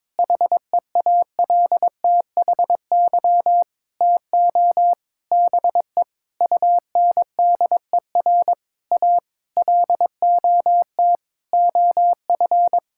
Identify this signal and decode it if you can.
Morse code, heard as a keyed tone, about 22 wpm, 700 Hz. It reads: HEALTHY TO BE UNDER A LOT OF
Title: Sentences from Top 1000 Words - Review Part 6 - 22wpm